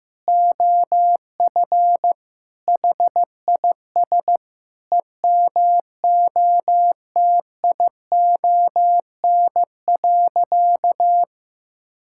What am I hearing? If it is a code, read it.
Morse code, heard as a keyed tone, about 15 wpm, 700 Hz. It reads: OF HIS EMOTION.
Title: Sentences from Top 1600 Words 15wpm